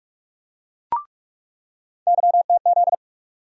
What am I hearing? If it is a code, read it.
Morse code, heard as a keyed tone, about 44 wpm, 700 Hz. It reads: YTZ
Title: USA Call Sign Postfixes - 3 Letters 45wpm